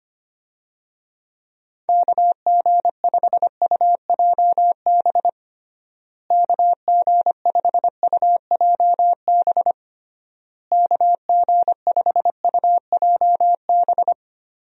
Morse code, heard as a keyed tone, about 25 wpm, 700 Hz. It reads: KG5UJB KG5UJB KG5UJB
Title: USA Call Signs - 6 Characters - Repeated 3x - 25wpm